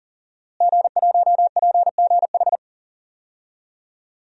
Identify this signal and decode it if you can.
Morse code, heard as a keyed tone, about 40 wpm, 700 Hz. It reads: G1PGH
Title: DXCC-100 Call Signs - 5 Characters - Repeated 2x - 40wpm